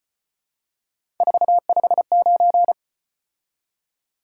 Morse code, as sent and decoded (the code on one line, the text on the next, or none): ....- ..... ----.
459